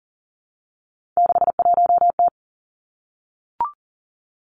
Morse code, heard as a keyed tone, about 38 wpm, 700 Hz. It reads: 61T
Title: Three Challenging Letter-Number Combinations 40wpm